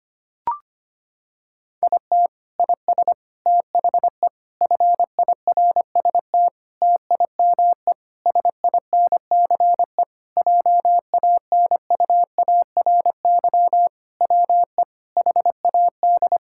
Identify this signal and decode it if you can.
Morse code, heard as a keyed tone, about 25 wpm, 700 Hz. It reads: IT IS THE FIRST TIME SINCE JANUARY WE HAD
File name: practice / 25wpm / Sentences with 4-Word Phrases - 25wpm